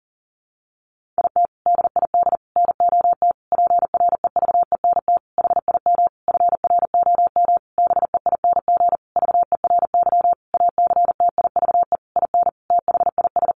Morse code, heard as a keyed tone, about 40 wpm, 700 Hz. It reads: IT DID NOT PREVENT HIM FROM BEING VERY ACTIVE IN THIS
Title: Sentences from Top 1500 Words 40wpm